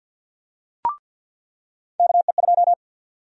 Morse code, heard as a keyed tone, about 50 wpm, 700 Hz. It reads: KE2
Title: USA Call Sign Prefixes - 50wpm